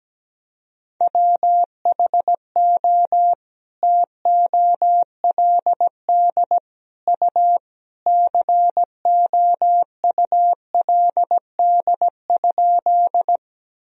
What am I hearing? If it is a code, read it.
Morse code, heard as a keyed tone, about 17 wpm, 700 Hz. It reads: WHO TOLD U COULD?
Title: Sentences from Top 400 Words and Abbreviations 17wpm